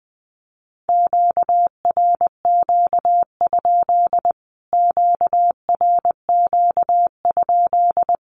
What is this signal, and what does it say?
Morse code, heard as a keyed tone, about 20 wpm, 700 Hz. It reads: QRQ? QRQ?
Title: Q-Codes Repeated 3x 20wpm